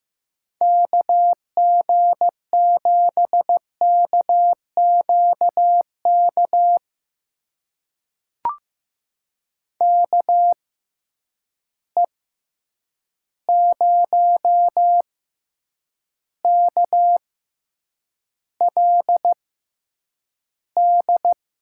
Morse code, heard as a keyed tone, about 15 wpm, spaced as for 8 wpm, 700 Hz. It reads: KG7KQK KE0KLD
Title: USA Call Signs - 6 Characters - 6x Character Spacing - 15wpm